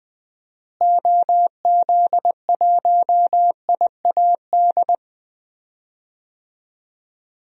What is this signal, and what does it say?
Morse code, heard as a keyed tone, about 20 wpm, 700 Hz. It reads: OZ1IAD
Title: DXCC-100 Call Signs - 6 Characters - Repeated 2x - 20wpm